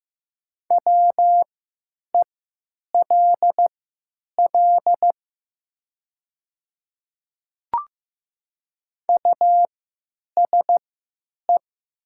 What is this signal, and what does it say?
Morse code, heard as a keyed tone, about 15 wpm, spaced as for 9 wpm, 700 Hz. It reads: WELL USE
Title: 3x Character Spacing - Top 200 Words - 15wpm